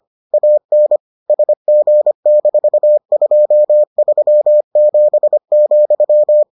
Morse code, heard as a keyed tone, about 25 wpm, 600 Hz. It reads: AN SG-237,